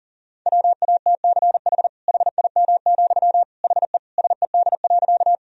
Morse code, heard as a keyed tone, about 40 wpm, 700 Hz. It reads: WATCH HIM, HE SED.
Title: Sentences from Top 300 Words and Abbreviations 40wpm